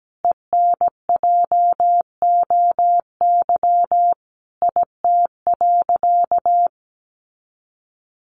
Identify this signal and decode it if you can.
Morse code, heard as a keyed tone, about 17 wpm, 700 Hz. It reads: ENJOY IT.
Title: Sentences from Top 1500 Words 17wpm